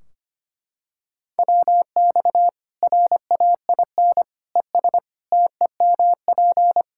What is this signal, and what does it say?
Morse code, heard as a keyed tone, about 25 wpm, 700 Hz. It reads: WX RAIN ES TEMP
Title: Practice QSO - Exchange 2 - Info - 25wpm